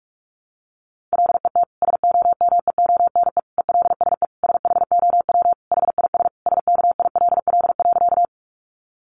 Morse code, heard as a keyed tone, about 44 wpm, 700 Hz. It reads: LET SOMEONE ELSE SHOW HIS SKILL.
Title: Sentences from Top 1200 Words 45wpm